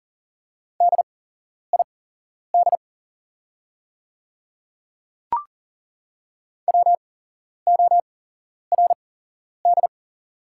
Morse code, heard as a keyed tone, about 40 wpm, spaced as for 11 wpm, 690 Hz. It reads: DID WORD
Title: 8x Character Spacing - Top 200 Words - 40wpm